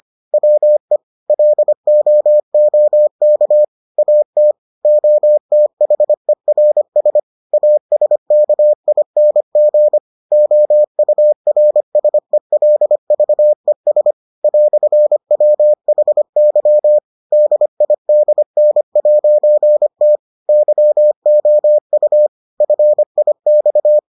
Morse code, heard as a keyed tone, about 25 wpm, 600 Hz. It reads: WE LOOK AT OTHERS ASKING OURSELVES "WHY DIDN'T YOU FIX